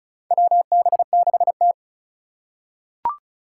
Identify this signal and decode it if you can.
Morse code, heard as a keyed tone, about 36 wpm, 700 Hz. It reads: WB6T
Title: USA Call Signs - 3x Character Spacing - 35wpm